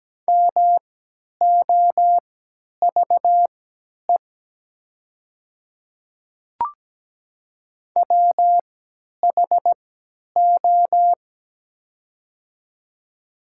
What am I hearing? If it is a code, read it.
Morse code, heard as a keyed tone, about 17 wpm, spaced as for 10 wpm, 700 Hz. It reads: MOVE WHO
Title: 3x Character Spacing - Top 200 Words - 17wpm